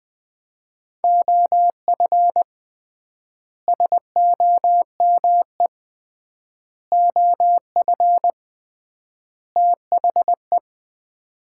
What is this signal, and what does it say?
Morse code, heard as a keyed tone, about 21 wpm, 700 Hz. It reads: OF SOME OF THE
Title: 3x Word-Spacing - Sets of 4 Words - 20wpm